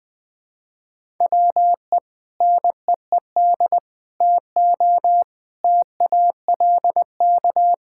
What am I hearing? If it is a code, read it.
Morse code, heard as a keyed tone, about 20 wpm, 700 Hz. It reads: WE NEED TO TALK
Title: Sentences with 2-Word Phrases - 20wpm